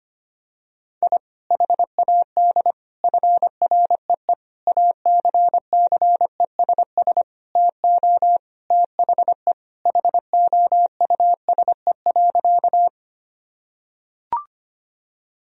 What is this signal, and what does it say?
Morse code, heard as a keyed tone, about 25 wpm, 700 Hz. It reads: I HAD FREE ACCESS TO THE HOUSE.
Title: Sentences from Top 1000 Words - Review Part 4 - 25wpm